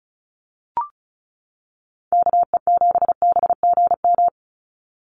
Morse code, heard as a keyed tone, about 34 wpm, 700 Hz. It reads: KE7BGM
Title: NA SOTA Activator Callsigns 35wpm